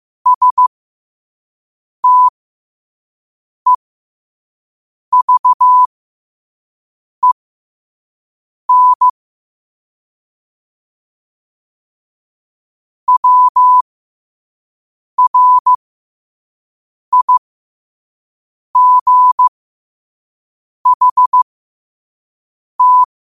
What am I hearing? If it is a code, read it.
Morse code, heard as a keyed tone, about 15 wpm, spaced as for 5 wpm, 1000 Hz. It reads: STEVEN WRIGHT
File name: qotdfiles / 5wpm